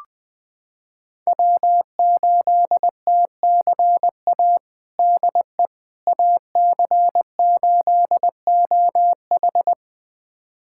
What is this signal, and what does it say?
Morse code, heard as a keyed tone, about 21 wpm, 700 Hz. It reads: W8TCA DE AC8OH
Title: USA Callsign DE Callsign - 5 Characters - 20wpm